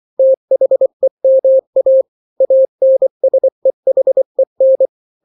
Morse code, heard as a keyed tone, about 24 wpm, 530 Hz. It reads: THEMA ANSEHEN
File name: daily / 23wpm